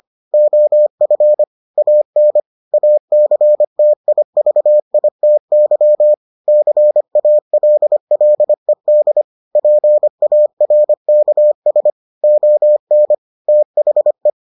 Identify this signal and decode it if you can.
Morse code, heard as a keyed tone, about 25 wpm, 600 Hz. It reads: OF AN ACTIVITY CALLED PARKS ON THE